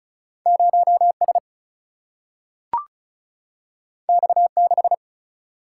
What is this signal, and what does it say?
Morse code, heard as a keyed tone, about 36 wpm, 700 Hz. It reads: 0S X6
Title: Single-Letter Single-Number 35wpm